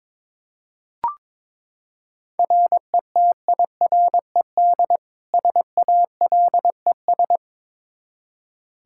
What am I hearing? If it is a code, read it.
Morse code, heard as a keyed tone, about 21 wpm, 700 Hz. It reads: RETIRED SALES
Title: Practice QSO - Exchange 3 - More Info - 22wpm